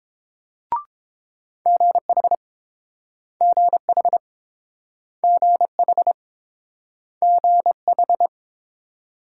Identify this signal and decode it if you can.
Morse code, heard as a keyed tone, about 29 wpm, 700 Hz. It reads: GH GH GH GH